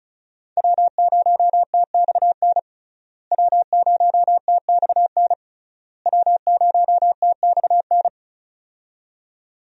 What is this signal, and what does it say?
Morse code, heard as a keyed tone, about 36 wpm, 700 Hz. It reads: W0TXN W0TXN W0TXN
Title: USA Call Signs - 5 Characters - Repeated 3x - 35wpm